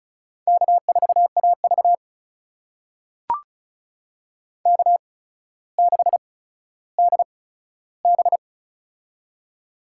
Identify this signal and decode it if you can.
Morse code, heard as a keyed tone, about 36 wpm, spaced as for 15 wpm, 700 Hz. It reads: K4AV K6DB
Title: USA Call Signs - 4 Characters - 8x Character Spacing - 35wpm